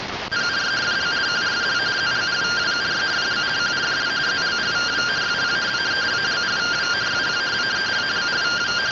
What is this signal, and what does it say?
Two-tone teleprinter signal, about 170 Hz shift.